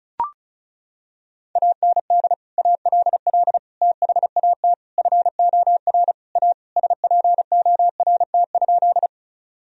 Morse code, heard as a keyed tone, about 36 wpm, 700 Hz. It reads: AND ALL THAT FOR A SPORT?
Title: Sentences from Top 1000 Words - Review Part 4 - 35wpm